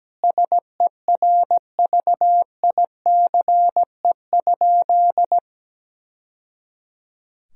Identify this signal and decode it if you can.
Morse code, heard as a keyed tone, about 17 wpm, 700 Hz. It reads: SERVICE?
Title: Sentences from Top 1000 Words - Review Part 4 - 17wpm